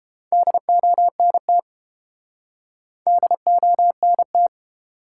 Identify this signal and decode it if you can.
Morse code, heard as a keyed tone, about 30 wpm, 700 Hz. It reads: DONT DONT